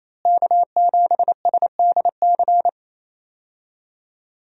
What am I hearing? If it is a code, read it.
Morse code, heard as a keyed tone, about 28 wpm, 700 Hz. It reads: K7SDC
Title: USA Call Signs - 5 Characters - Repeated 3x - 28wpm